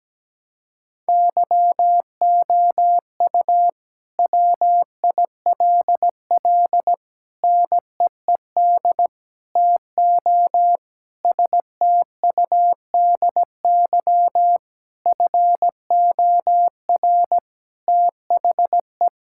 Morse code, heard as a keyed tone, about 17 wpm, 700 Hz. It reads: YOU WILL NEED TO STUDY FOR THE
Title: Sentences from Top 1000 Words - Review Part 5 - 17wpm